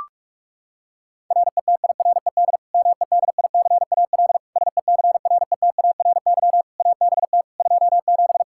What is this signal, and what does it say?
Morse code, heard as a keyed tone, about 41 wpm, 700 Hz. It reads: RETIRED MEDICAL SECRETARY ABT 27
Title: Practice QSO - Exchange 3 - More Info - 45wpm